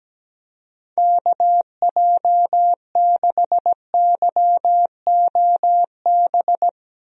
Morse code, heard as a keyed tone, about 17 wpm, 700 Hz. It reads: KJ6YOB
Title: USA Call Signs - 6 Characters - Repeated 2x - 17wpm